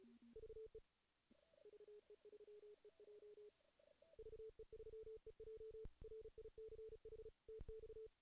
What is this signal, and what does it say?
Morse code, heard as a keyed tone, about 34 wpm, 440 Hz. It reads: VE VE3EJ VE3EJ RICH TX
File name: 2016 JAN CW NAQP / 2016 Jan naqp k5ot - 0-2